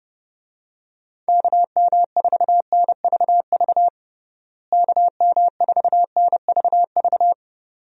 Morse code, heard as a keyed tone, about 31 wpm, 700 Hz. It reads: KM4NVV KM4NVV
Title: USA Call Signs - 6 Characters - Repeated 2x - 30wpm